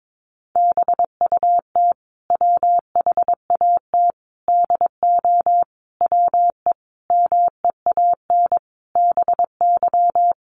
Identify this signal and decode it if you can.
Morse code, heard as a keyed tone, about 22 wpm, 700 Hz. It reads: BUT WHAT DO WE MEAN BY
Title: Sentences from Top 1100 Words 22wpm